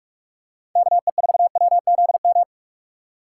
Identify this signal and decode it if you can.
Morse code, heard as a keyed tone, about 44 wpm, 690 Hz. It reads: KE4WZM